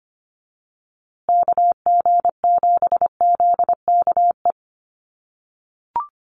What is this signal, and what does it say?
Morse code, heard as a keyed tone, about 25 wpm, 700 Hz. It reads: KG7ZKE
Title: USA Call Signs Repeated 3x 25wpm